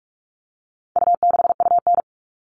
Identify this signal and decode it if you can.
Morse code, heard as a keyed tone, about 44 wpm, 700 Hz. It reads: U6UN
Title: All Cut Numbers - 4-Digit 45wpm